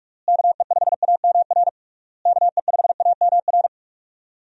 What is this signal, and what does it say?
Morse code, heard as a keyed tone, about 48 wpm, 700 Hz. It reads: KE5AMR KE5AMR
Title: USA Call Signs - 6 Characters - Repeated 3x - 45wpm